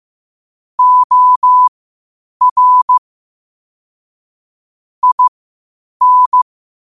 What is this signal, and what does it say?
Morse code, heard as a keyed tone, about 15 wpm, spaced as for 8 wpm, 1000 Hz. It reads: OR IN
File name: qotdfiles / 8wpm